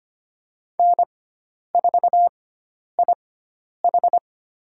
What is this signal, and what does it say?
Morse code, heard as a keyed tone, about 25 wpm, spaced as for 10 wpm, 700 Hz. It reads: N4IH
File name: Lesson 16g - Letter c - Call Signs 25wpm